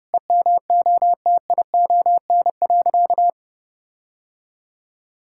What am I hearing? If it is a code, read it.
Morse code, heard as a keyed tone, about 30 wpm, 700 Hz. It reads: EMOTION.
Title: Sentences from Top 1600 Words 30wpm